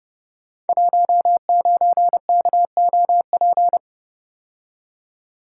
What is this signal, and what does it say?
Morse code, heard as a keyed tone, about 30 wpm, 700 Hz. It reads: 19KOP
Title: Code Groups - 5 Letter:Numbers - 30wpm